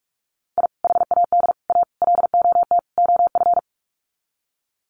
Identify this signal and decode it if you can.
Morse code, heard as a keyed tone, about 44 wpm, 700 Hz. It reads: I HAD A LOT OF
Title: Sets of 5 Words Encore 45wpm